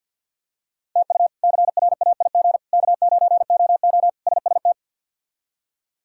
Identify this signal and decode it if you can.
Morse code, heard as a keyed tone, about 47 wpm, 700 Hz. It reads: TU CRAIG K9OO SST
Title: K1USN SST QSO Practice - 50wpm